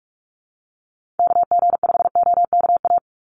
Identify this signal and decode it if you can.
Morse code, heard as a keyed tone, about 44 wpm, 700 Hz. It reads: KG5OKA
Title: USA Call Signs 45wpm